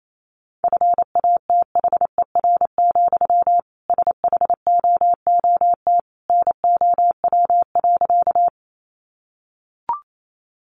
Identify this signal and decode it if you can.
Morse code, heard as a keyed tone, about 28 wpm, 700 Hz. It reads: FATHER, SHOOT NOW.
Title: Sentences from Top 1200 Words 28wpm